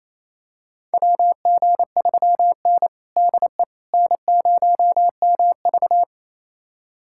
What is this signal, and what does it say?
Morse code, heard as a keyed tone, about 28 wpm, 700 Hz. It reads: WG3N DE N0MV
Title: USA Callsign DE Callsign - 4 Characters - 28wpm